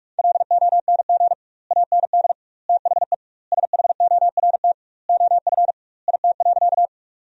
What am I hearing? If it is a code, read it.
Morse code, heard as a keyed tone, about 46 wpm, 700 Hz. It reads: LONG AND THE SHORT OF IT.
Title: 2x Word-Spacing - Sentences from Top 400 Words - 45wpm